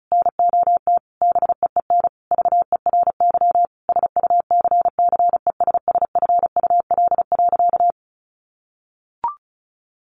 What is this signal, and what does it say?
Morse code, heard as a keyed tone, about 34 wpm, 700 Hz. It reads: NOT BEEN VERY SUCCESSFUL.
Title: Sentences from Top 1300 Words 35wpm